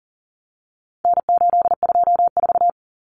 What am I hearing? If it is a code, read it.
Morse code, heard as a keyed tone, about 40 wpm, 700 Hz. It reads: N824
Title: Common Cut Numbers - 4-Digit 40wpm